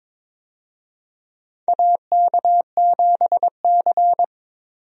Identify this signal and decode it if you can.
Morse code, heard as a keyed tone, about 22 wpm, 700 Hz. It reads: AK7C
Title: USA Call Signs - 4 Characters - 22wpm